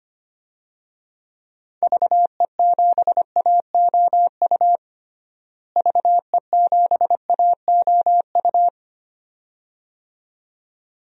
Morse code, heard as a keyed tone, about 25 wpm, 700 Hz. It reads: VE7AOU VE7AOU
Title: DXCC-100 Call Signs - 6 Characters - Repeated 2x - 25wpm